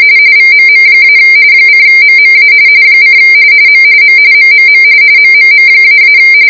RTTY, about 45.45 baud. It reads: VMM- :MMA : M M MMM MMM--